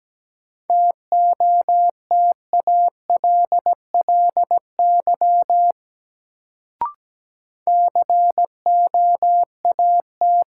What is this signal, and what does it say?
Morse code, heard as a keyed tone, about 17 wpm, 700 Hz. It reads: TOTALLY COAT
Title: Sentences from Top 1600 Words - New Word Drill Alt - 17wpm